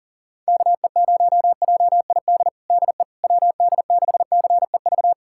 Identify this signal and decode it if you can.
Morse code, heard as a keyed tone, about 37 wpm, 700 Hz. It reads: KE0JID DE WD6CEV
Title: USA Callsign DE Callsign - 6 Characters - 40wpm